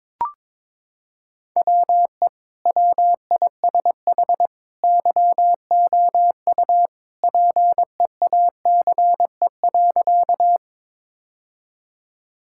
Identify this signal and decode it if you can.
Morse code, heard as a keyed tone, about 22 wpm, 700 Hz. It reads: WE WISH YOU PEACE.
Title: Sentences from Top 1600 Words 22wpm